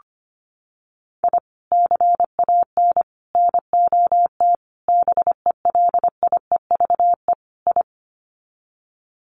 Morse code, heard as a keyed tone, about 25 wpm, 700 Hz. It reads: I CAN NOT BELIEVE I
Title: Sets of 5 Words Encore2 25wpm